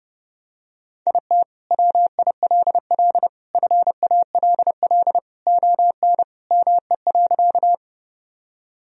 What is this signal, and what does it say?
Morse code, heard as a keyed tone, about 32 wpm, 700 Hz. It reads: IT WILL FALL ON ME.
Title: Sentences from Top 1000 Words - Review Part 2 - 30wpm